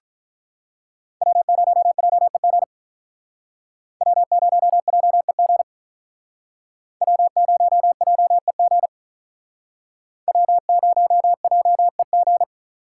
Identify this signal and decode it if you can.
Morse code, heard as a keyed tone, about 43 wpm, 700 Hz. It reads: W0JEG W0JEG W0JEG W0JEG
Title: Speed Racing - USA 1x3 Call Signs 35wpm